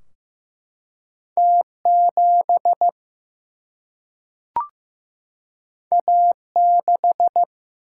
Morse code, heard as a keyed tone, about 15 wpm, 700 Hz. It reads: T7 A6
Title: Single-Letter Single-Number 15wpm